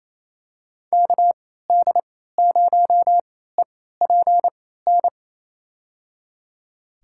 Morse code, heard as a keyed tone, about 28 wpm, spaced as for 16 wpm, 700 Hz. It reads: KD0EPN
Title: USA Call Signs - 6 Characters - 3x Character Spacing - 28wpm